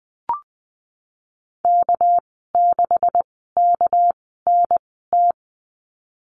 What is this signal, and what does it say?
Morse code, heard as a keyed tone, about 20 wpm, spaced as for 14 wpm, 700 Hz. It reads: K6KNT
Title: 200 Lesson 32g - Letter k - Call Signs 20wpm